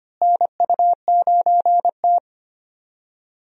Morse code, heard as a keyed tone, about 25 wpm, 700 Hz. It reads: NU9T
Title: USA Call Signs - 4 Characters - Repeated 2x - 25wpm